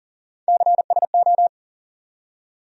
Morse code, heard as a keyed tone, about 40 wpm, 700 Hz. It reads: CSO
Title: USA Call Sign Postfixes - 3 Letters 40wpm